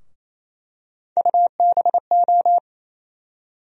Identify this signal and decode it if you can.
Morse code, heard as a keyed tone, about 28 wpm, 700 Hz. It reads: UBO
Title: USA Call Sign Postfixes - 3 Letters 28wpm